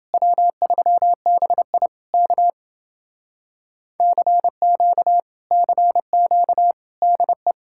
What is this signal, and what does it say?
Morse code, heard as a keyed tone, about 29 wpm, 700 Hz. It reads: W3BI K CQ CQ DE